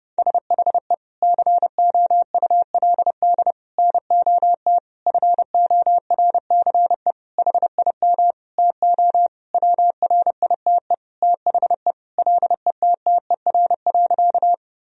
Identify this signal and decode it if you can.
Morse code, heard as a keyed tone, about 30 wpm, 700 Hz. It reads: SHE COULD NOT FORCE HIM TO WRITE THE LETTER.
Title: Sentences from Top 1000 Words - Review Part 3 - 30wpm